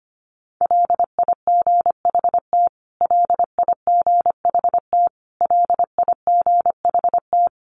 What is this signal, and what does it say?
Morse code, heard as a keyed tone, about 25 wpm, 700 Hz. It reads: LIGHT LIGHT LIGHT
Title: Top 200 Words Repeated 25wpm